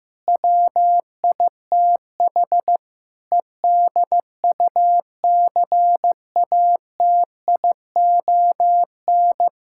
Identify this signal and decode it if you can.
Morse code, heard as a keyed tone, about 16 wpm, 700 Hz. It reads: WITH EDUCATION
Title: Sentences from Top 1000 Words - Review Part 6 - 15wpm